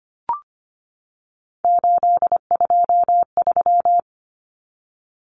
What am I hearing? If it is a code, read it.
Morse code, heard as a keyed tone, about 25 wpm, 700 Hz. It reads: 823
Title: Common Cut Numbers - 3-Digit 25wpm